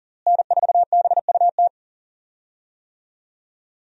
Morse code, heard as a keyed tone, about 40 wpm, 700 Hz. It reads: N4BUT
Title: USA Call Signs - 5 Characters - Repeated 3x - 40wpm